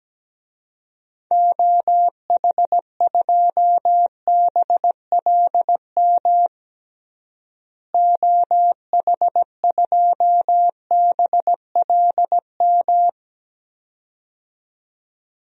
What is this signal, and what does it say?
Morse code, heard as a keyed tone, about 17 wpm, 700 Hz. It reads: OH2BLM OH2BLM
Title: DXCC-100 Call Signs - 6 Characters - Repeated 2x - 17wpm